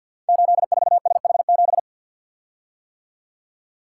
Morse code, heard as a keyed tone, about 50 wpm, 700 Hz. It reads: 84SH7
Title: Code Groups - 5 Letter:Numbers - 50wpm